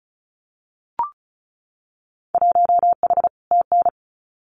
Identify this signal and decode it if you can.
Morse code, heard as a keyed tone, about 34 wpm, 690 Hz. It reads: 1H TN
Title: Winter Field Day Exchanges 35wpm